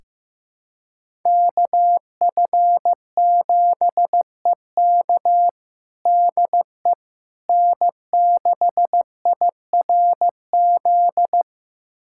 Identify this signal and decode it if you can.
Morse code, heard as a keyed tone, about 15 wpm, 700 Hz. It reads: KF7EK DE N6IRZ
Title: USA Callsign DE Callsign - 5 Characters - 15wpm